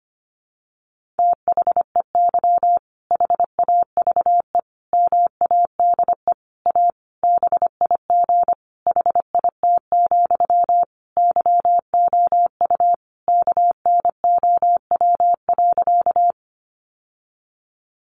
Morse code, heard as a keyed tone, about 25 wpm, 700 Hz. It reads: THEY HAVE MADE A BIG HIT, YOU KNOW.
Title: Sentences from Top 1200 Words 25wpm